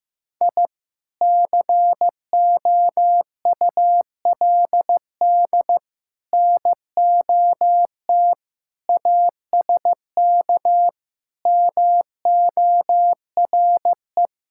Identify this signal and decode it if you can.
Morse code, heard as a keyed tone, about 15 wpm, 700 Hz. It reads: I COULD NOT ASK MORE
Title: Sentences from Top 1000 Words - Review Part 5 - 15wpm